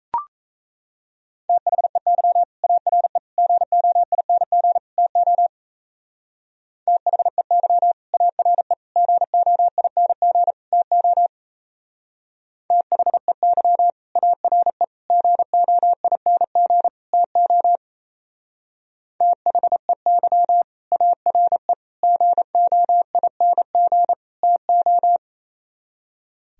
Morse code, heard as a keyed tone, about 36 wpm, 700 Hz. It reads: THEY ARE GOING TO THEY ARE GOING TO THEY ARE GOING TO THEY ARE GOING TO